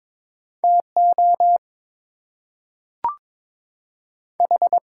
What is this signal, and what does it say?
Morse code, heard as a keyed tone, about 22 wpm, 700 Hz. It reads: TO H
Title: Two Letter Words - 22wpm